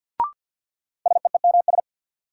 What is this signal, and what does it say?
Morse code, heard as a keyed tone, about 51 wpm, 700 Hz. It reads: SEEMS
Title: Sets of 3 Words - New Word Drill - 50wpm